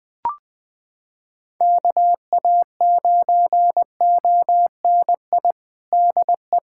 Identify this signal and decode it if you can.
Morse code, heard as a keyed tone, about 21 wpm, 700 Hz. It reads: KA9ONI DE
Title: USA Callsign DE Callsign - 6 Characters - 20wpm